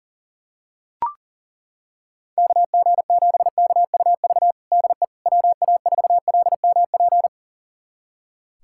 Morse code, heard as a keyed tone, about 37 wpm, 700 Hz. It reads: KG7KUV DE WA4LMP
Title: USA Callsign DE Callsign - 6 Characters - 40wpm